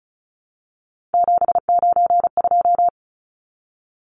Morse code, heard as a keyed tone, about 34 wpm, 700 Hz. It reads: 792
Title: Numbers - 3-Digit 35wpm